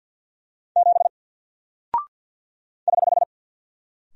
Morse code, heard as a keyed tone, about 49 wpm, 700 Hz. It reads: 7 <HH>
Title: Single Character (Letter:Number:Punctuation:Prosign) Alt 50wpm